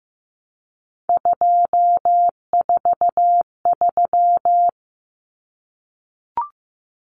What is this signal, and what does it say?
Morse code, heard as a keyed tone, about 15 wpm, 700 Hz. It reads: 243
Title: Common Cut Numbers - 3-Digit 15wpm